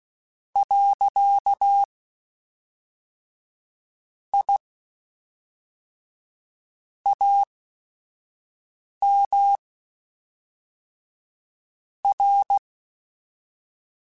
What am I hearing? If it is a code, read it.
Morse code, heard as a keyed tone, about 16 wpm, spaced as for 5 wpm, 800 Hz. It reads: . I AM R
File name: QSOs / QSOs-mp3s / kc7sac@5